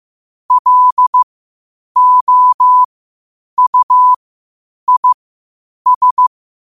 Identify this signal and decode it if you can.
Morse code, heard as a keyed tone, about 15 wpm, spaced as for 8 wpm, 1000 Hz. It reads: LOUIS
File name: qotdfiles / 8wpm